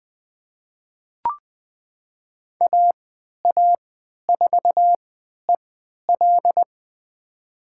Morse code, heard as a keyed tone, about 21 wpm, spaced as for 11 wpm, 700 Hz. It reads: AA4EL
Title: USA Call Signs - 5 Characters - 3x Character Spacing - 20wpm